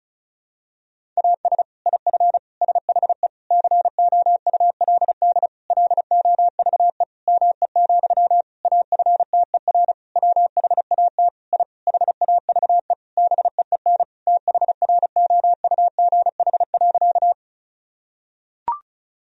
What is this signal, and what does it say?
Morse code, heard as a keyed tone, about 36 wpm, 700 Hz. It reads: AS IF SHE COULD LOVE ME, AFTER WHAT I HAVE BEEN THROUGH.
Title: Sentences from Top 1000 Words - Review Part 1 - 35wpm